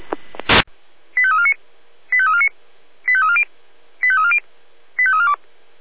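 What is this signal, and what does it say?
Five-tone selective call (ZVEI), heard as quick runs of five tones, tones of about 80 ms. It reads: 85349 85349 85340 85340 85342
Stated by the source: ZVEI five-tone selective call received on SDR.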